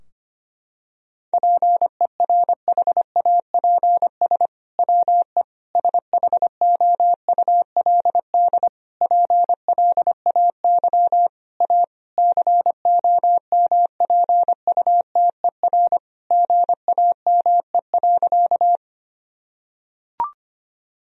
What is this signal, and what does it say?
Morse code, heard as a keyed tone, about 25 wpm, 700 Hz. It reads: PERHAPS WE SHOULD PLAY A COMPUTER GAME.
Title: Sentences from Top 1000 Words - Review Part 3 - 25wpm